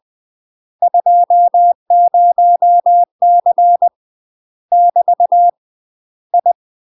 Morse code, heard as a keyed tone, about 20 wpm, 700 Hz. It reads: 20C <BT> I
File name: UK MASTER QSOs TRAINING / 20WPM / UK MASTER QSOs 0059